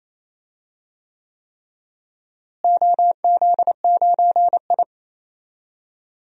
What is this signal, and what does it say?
Morse code, heard as a keyed tone, about 28 wpm, 700 Hz. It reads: OZ9I
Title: DXCC-100 Call Signs - 4 Characters - Repeated 3x - 28wpm